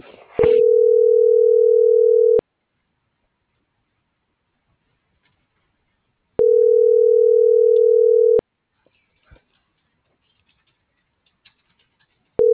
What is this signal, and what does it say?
Ringback tone (North America), 440 + 480 Hz.